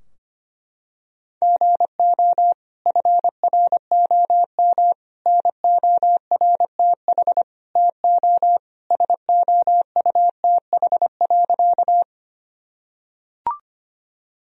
Morse code, heard as a keyed tone, about 25 wpm, 700 Hz. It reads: GO FROM NORTH TO SOUTH.
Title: Sentences from Top 1000 Words - Review Part 2 - 25wpm